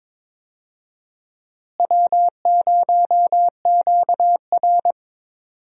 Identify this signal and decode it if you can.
Morse code, heard as a keyed tone, about 22 wpm, 700 Hz. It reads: W0QR
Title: USA Call Signs - 4 Characters - Repeated 2x - 22wpm